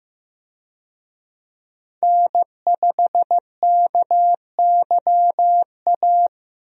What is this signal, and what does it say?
Morse code, heard as a keyed tone, about 14 wpm, 700 Hz. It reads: N5KYA
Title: USA Call Signs - 5 Characters - Repeated 2x - 15wpm